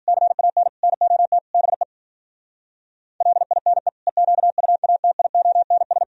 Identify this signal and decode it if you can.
Morse code, heard as a keyed tone, about 52 wpm, 700 Hz. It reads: CAN NOT BE LINE EQUATIONS